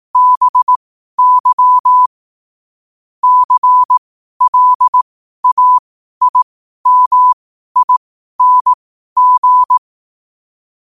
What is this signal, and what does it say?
Morse code, heard as a keyed tone, about 18 wpm, spaced as for 12 wpm, 1000 Hz. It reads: BY CLAIMING
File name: qotdfiles / 12wpm